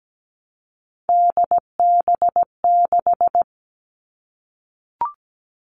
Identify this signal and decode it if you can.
Morse code, heard as a keyed tone, about 17 wpm, 700 Hz. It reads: DB6
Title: All Cut Numbers - 3-Digit 17wpm